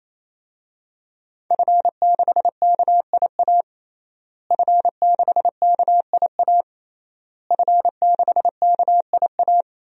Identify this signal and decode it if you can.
Morse code, heard as a keyed tone, about 27 wpm, 700 Hz. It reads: F6KIA F6KIA F6KIA
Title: DXCC-100 Call Signs - 5 Characters - Repeated 3x - 28wpm